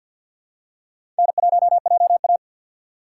Morse code, heard as a keyed tone, about 49 wpm, 700 Hz. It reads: N1JA
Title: USA Call Signs - 4 Characters - 3x Character Spacing - 50wpm